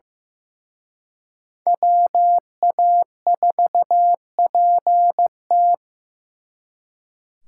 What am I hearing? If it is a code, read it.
Morse code, heard as a keyed tone, about 18 wpm, 700 Hz. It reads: WA4PT
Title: Speed Racing - USA 2x2 Call Signs 15wpm